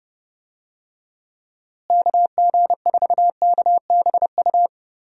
Morse code, heard as a keyed tone, about 30 wpm, 700 Hz. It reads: KG4KBU
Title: USA Call Signs - 6 Characters - Repeated 2x - 30wpm